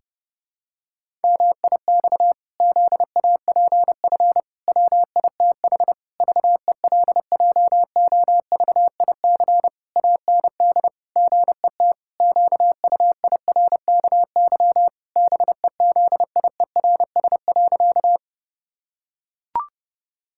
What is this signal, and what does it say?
Morse code, heard as a keyed tone, about 30 wpm, 700 Hz. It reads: MIX ZAPF WITH VELJOVIC AND GET QUIRKY BEZIERS.
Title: Pangram Challenge v2 - 30wpm